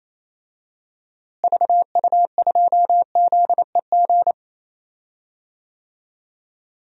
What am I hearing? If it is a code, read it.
Morse code, heard as a keyed tone, about 28 wpm, 700 Hz. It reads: VU2ZEG
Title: DXCC-100 Call Signs - 6 Characters - Repeated 3x - 28wpm